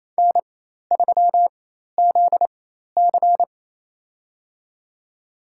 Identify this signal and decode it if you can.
Morse code, heard as a keyed tone, about 28 wpm, spaced as for 13 wpm, 700 Hz. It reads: N3ZC
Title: USA Call Signs - 4 Characters - 4x Character Spacing - 28wpm